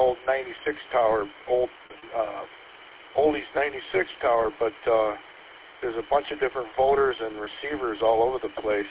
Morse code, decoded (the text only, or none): none